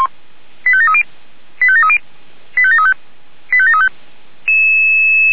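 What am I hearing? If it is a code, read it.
Five-tone selective call (ZVEI), heard as quick runs of five tones, tones of about 70 ms. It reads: 85730 85730 85735 85735
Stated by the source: ZVEI five-tone selective call received on an SDR.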